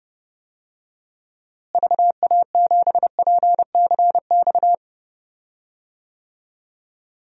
Morse code, heard as a keyed tone, about 30 wpm, 700 Hz. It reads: VA7PCX
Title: DXCC-100 Call Signs - 6 Characters - Repeated 3x - 30wpm